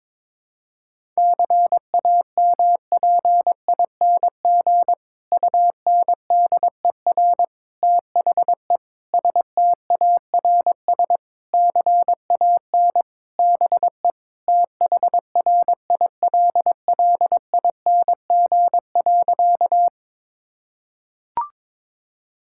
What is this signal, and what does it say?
Morse code, heard as a keyed tone, about 22 wpm, 700 Hz. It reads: CAMPING UNDER THE STARS CAN BE THRILLING.
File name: Difficult Sentences with Hints - 22wpm